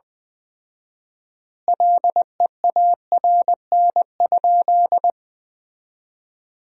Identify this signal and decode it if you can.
Morse code, heard as a keyed tone, about 21 wpm, 700 Hz. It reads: LEARN?
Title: 4x Word-Spacing - Sentences from Top 300 Words - 20wpm